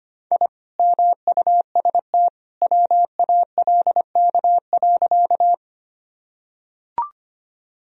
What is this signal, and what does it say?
Morse code, heard as a keyed tone, about 25 wpm, 700 Hz. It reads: I MUST WALK.
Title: Sentences from Top 1000 Words - Review Part 1 - 25wpm